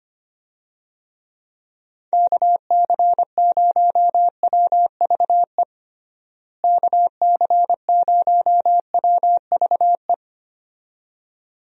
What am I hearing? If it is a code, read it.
Morse code, heard as a keyed tone, about 25 wpm, 700 Hz. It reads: KC0WVE KC0WVE
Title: USA Call Signs - 6 Characters - Repeated 2x - 25wpm